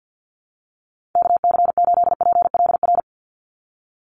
Morse code, heard as a keyed tone, about 50 wpm, 700 Hz. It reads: KC8PLR
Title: USA Call Signs 50wpm